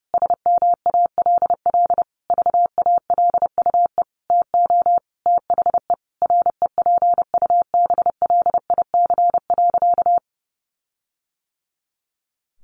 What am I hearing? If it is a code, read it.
Morse code, heard as a keyed tone, about 30 wpm, 700 Hz. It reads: SMALL VALUE TO THE REPUBLIC.
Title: Sentences from Top 1500 Words 30wpm